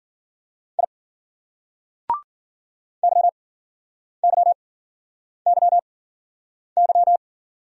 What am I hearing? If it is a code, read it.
Morse code, heard as a keyed tone, about 45 wpm, spaced as for 9 wpm, 700 Hz. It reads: I YYYY
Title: Speed-Racing Single Letter-Number 40wpm